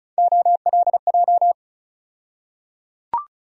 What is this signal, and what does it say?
Morse code, heard as a keyed tone, about 36 wpm, 700 Hz. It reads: OLJ
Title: USA Call Sign Postfixes - 3 Letters 35wpm